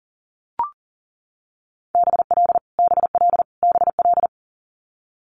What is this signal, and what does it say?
Morse code, heard as a keyed tone, about 41 wpm, 690 Hz. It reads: BL BL BL
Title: Multi-letter Phonemes Repeated 3x 40wpm